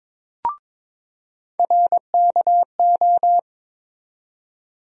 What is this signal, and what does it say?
Morse code, heard as a keyed tone, about 21 wpm, 700 Hz. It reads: RKO
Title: USA Call Sign Postfixes - 3 Letters 22wpm